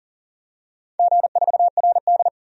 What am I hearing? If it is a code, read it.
Morse code, heard as a keyed tone, about 40 wpm, 700 Hz. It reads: G4RD
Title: DXCC-100 Call Signs - 4 Characters - 40wpm